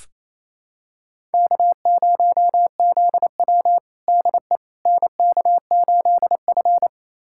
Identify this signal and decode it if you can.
Morse code, heard as a keyed tone, about 28 wpm, 700 Hz. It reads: K0ZW DE NK8F
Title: USA Callsign DE Callsign - 4 Characters - 28wpm